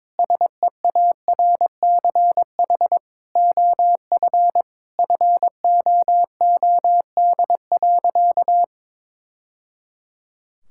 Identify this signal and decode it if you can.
Morse code, heard as a keyed tone, about 22 wpm, 700 Hz. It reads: SEARCH OF FOOD.
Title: Sentences from Top 1000 Words - Review Part 5 - 22wpm